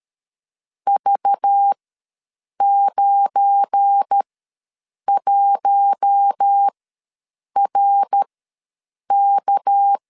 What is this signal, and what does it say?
Morse code, heard as a keyed tone, about 13 wpm, spaced as for 7 wpm, 800 Hz. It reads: V91RK
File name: LEZIONI / 8 YJ45 / 8 YJ45 ws 12wpm 3146